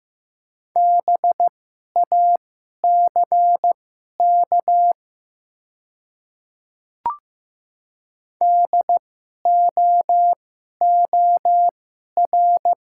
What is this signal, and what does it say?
Morse code, heard as a keyed tone, about 15 wpm, spaced as for 11 wpm, 700 Hz. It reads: BACK DOOR
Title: 2x Character Spacing - Top 500 Words - 15wpm